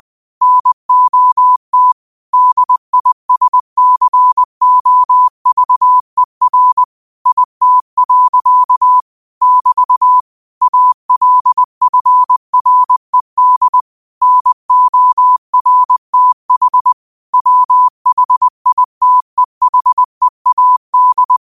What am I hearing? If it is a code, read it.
Morse code, heard as a keyed tone, about 20 wpm, 1000 Hz. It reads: NOT DISCOVER IT. <BT> ALFRED NORTH WHITEHEAD
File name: qotdfiles / 20wpm